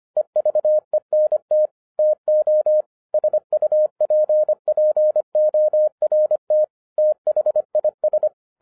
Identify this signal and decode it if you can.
Morse code, heard as a keyed tone, about 26 wpm, 610 Hz. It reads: EVENT TO SUPPORT THIS
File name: 2012-01-mp3 / n14-FUN WITH NUMBERS-25wpm